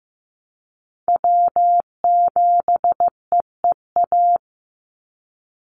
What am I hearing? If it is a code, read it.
Morse code, heard as a keyed tone, about 15 wpm, 700 Hz. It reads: W7EEA
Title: USA Call Signs 15wpm